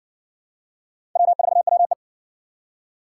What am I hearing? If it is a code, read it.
Morse code, heard as a keyed tone, about 53 wpm, 700 Hz. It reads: W4WE